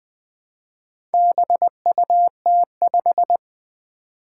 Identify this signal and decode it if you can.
Morse code, heard as a keyed tone, about 20 wpm, 700 Hz. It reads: BUT5
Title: Four Challenging Letter-Number Combinations 20wpm